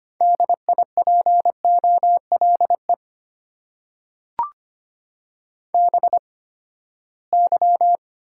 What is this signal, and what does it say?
Morse code, heard as a keyed tone, about 25 wpm, spaced as for 17 wpm, 700 Hz. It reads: DIPOLE BY
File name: Top 90 QSO Elements - 8x Character Spacing - 25wpm